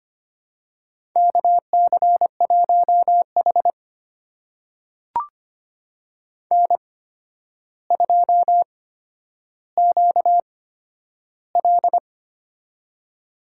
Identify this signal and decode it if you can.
Morse code, heard as a keyed tone, about 25 wpm, spaced as for 11 wpm, 700 Hz. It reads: KC1H N2QL
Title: USA Call Signs - 4 Characters - 8x Character Spacing - 25wpm